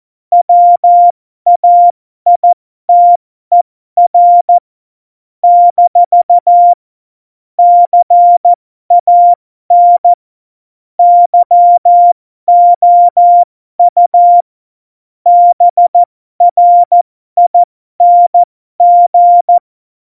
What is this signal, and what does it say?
Morse code, heard as a keyed tone, about 14 wpm, spaced as for 12 wpm, 700 Hz. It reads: WAITER - CAN YOU BRING
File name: Morsekurs / Witze